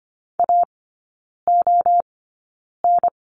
Morse code, heard as a keyed tone, about 25 wpm, spaced as for 9 wpm, 700 Hz. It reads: AON
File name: course / 25:10wpm / 093 Lesson 16e - Letter c - Three Characters 25wpm